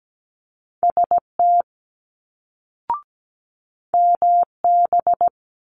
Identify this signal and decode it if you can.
Morse code, heard as a keyed tone, about 17 wpm, 700 Hz. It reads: ST MB
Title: Multi-letter Phonemes Quick Fire 17wpm